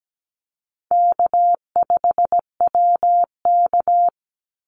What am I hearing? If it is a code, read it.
Morse code, heard as a keyed tone, about 17 wpm, 700 Hz. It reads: K5WK
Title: Easy Call Signs Repeated 2x - 17wpm